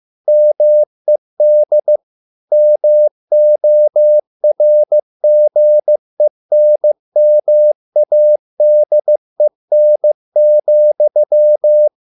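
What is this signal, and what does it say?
Morse code, heard as a keyed tone, about 15 wpm, 600 Hz. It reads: MED MORGENMADEN,
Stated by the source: Morse code at a slow 15 wpm from CW med Gnister.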